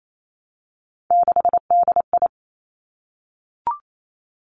Morse code, heard as a keyed tone, about 28 wpm, 700 Hz. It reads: 6DI
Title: Three Challenging Letter-Number Combinations 28wpm